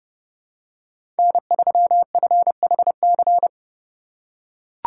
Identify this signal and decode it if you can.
Morse code, heard as a keyed tone, about 31 wpm, 700 Hz. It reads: N3FHC
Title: USA Call Signs - 5 Characters - 2x Character Spacing - 30wpm